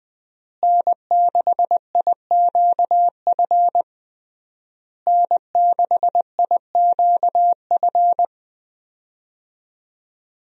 Morse code, heard as a keyed tone, about 20 wpm, 700 Hz. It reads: N6IQF N6IQF
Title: USA Call Signs - 5 Characters - Repeated 3x - 20wpm